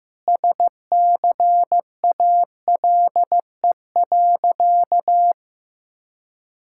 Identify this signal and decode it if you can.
Morse code, heard as a keyed tone, about 15 wpm, 700 Hz. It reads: SCALE.
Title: Sentences from Top 1000 Words - Review Part 5 - 15wpm